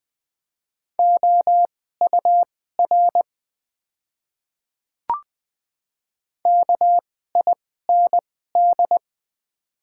Morse code, heard as a keyed tone, about 20 wpm, spaced as for 14 wpm, 700 Hz. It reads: OUR KIND
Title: 2x Character Spacing - Top 500 Words - 20wpm